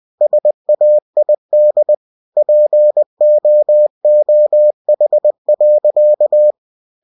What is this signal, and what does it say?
Morse code, heard as a keyed tone, about 20 wpm, 600 Hz. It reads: SAID POOH.
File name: cw-books / winnie-the-pooh / winnie-20wpm-page-0047